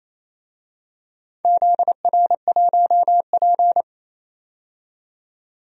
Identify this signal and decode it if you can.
Morse code, heard as a keyed tone, about 28 wpm, 700 Hz. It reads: ZR1P
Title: DXCC-100 Call Signs - 4 Characters - 28wpm